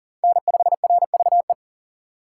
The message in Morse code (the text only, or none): N5RVE